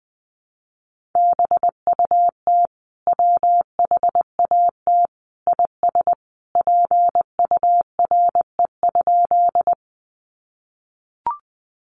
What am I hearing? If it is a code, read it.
Morse code, heard as a keyed tone, about 20 wpm, 700 Hz. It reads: BUT WHAT IS PURE?
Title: Sentences from Top 1500 Words 20wpm